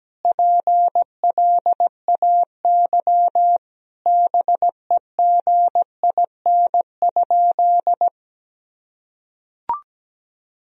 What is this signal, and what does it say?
Morse code, heard as a keyed tone, about 17 wpm, 700 Hz. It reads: PLAY BEGIN?
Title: Sentences from Top 1000 Words - Review Part 1 - 17wpm